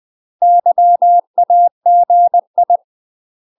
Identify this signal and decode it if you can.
Morse code, heard as a keyed tone, about 20 wpm, 700 Hz. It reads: YAGI
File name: UK MASTER QSOs TRAINING / 20WPM / UK MASTER QSOs 0040